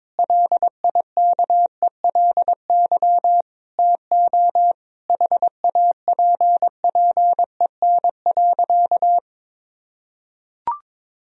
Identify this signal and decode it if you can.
Morse code, heard as a keyed tone, about 22 wpm, 700 Hz. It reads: LIKELY TO HAPPEN.
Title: Sentences from Top 1000 Words - Review Part 4 - 22wpm